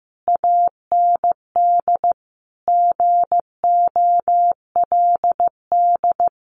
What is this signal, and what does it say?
Morse code, heard as a keyed tone, about 14 wpm, 700 Hz. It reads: AND GOLD
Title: A Sound Of Thunder 15wpm